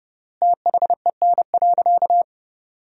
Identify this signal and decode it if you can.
Morse code, heard as a keyed tone, about 30 wpm, 700 Hz. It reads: THEN.
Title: 3x Word-Spacing - Sentences from Top 200 Words - 30wpm